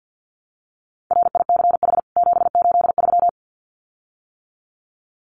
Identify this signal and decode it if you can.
Morse code, heard as a keyed tone, about 50 wpm, 700 Hz. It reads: RICH 783
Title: CWT Exchange 50wpm